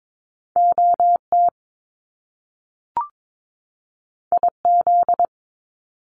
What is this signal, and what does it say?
Morse code, heard as a keyed tone, about 22 wpm, 700 Hz. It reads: OT IZ
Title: Top Two Letter Combinations in Words 22wpm